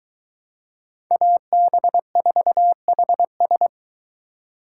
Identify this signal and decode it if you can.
Morse code, heard as a keyed tone, about 22 wpm, 700 Hz. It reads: AB4HS